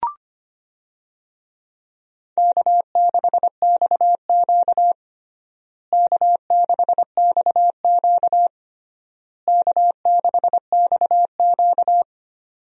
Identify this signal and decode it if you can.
Morse code, heard as a keyed tone, about 25 wpm, 700 Hz. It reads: K6XQ K6XQ K6XQ
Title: USA Call Signs - 4 Characters - Repeated 3x - 25wpm